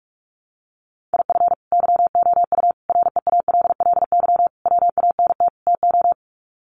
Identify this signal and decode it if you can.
Morse code, heard as a keyed tone, about 44 wpm, 700 Hz. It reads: IF YOU REALLY WANT TO
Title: Sets of 5 Words Encore2 45wpm